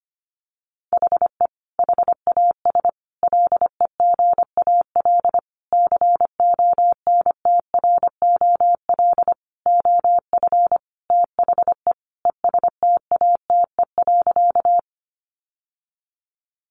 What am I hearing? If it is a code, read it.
Morse code, heard as a keyed tone, about 25 wpm, 700 Hz. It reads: HE HAS LEGAL CONTROL OF THE ESTATE.
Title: Sentences from Top 1200 Words 25wpm